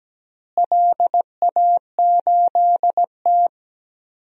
Morse code, heard as a keyed tone, about 17 wpm, 700 Hz. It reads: LA8T
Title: DXCC-100 Call Signs - 4 Characters - Repeated 3x - 17wpm